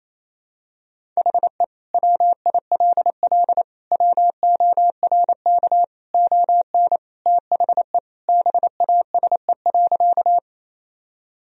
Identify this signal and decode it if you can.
Morse code, heard as a keyed tone, about 28 wpm, 700 Hz. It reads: HE WILL WORK ON THE BASE.
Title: Sentences from Top 1000 Words - Review Part 1 - 28wpm